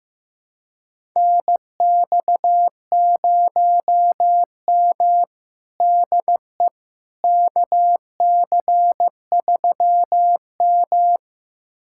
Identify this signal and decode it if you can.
Morse code, heard as a keyed tone, about 15 wpm, 700 Hz. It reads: NX0M DE KC3M
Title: USA Callsign DE Callsign - 4 Characters - 15wpm